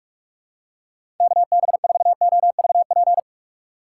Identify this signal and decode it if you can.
Morse code, heard as a keyed tone, about 46 wpm, 700 Hz. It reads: KB4OVP
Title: USA Call Signs - 2x Character Spacing - 45wpm